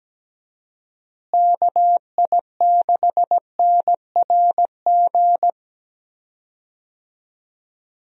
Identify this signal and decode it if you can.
Morse code, heard as a keyed tone, about 17 wpm, 700 Hz. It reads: KI6NRG
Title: USA Call Signs - 6 Characters - Repeated 3x - 17wpm